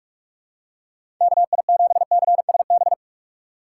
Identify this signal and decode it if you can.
Morse code, heard as a keyed tone, about 46 wpm, 700 Hz. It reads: KI7CSB
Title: USA Call Signs - 6 Characters - 4x Character Spacing - 45wpm